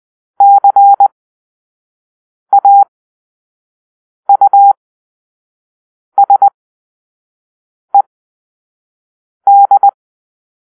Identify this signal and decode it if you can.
Morse code, heard as a keyed tone, about 20 wpm, spaced as for 5 wpm, 800 Hz. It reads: CAUSED
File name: arrl / Grp-5 Drill-4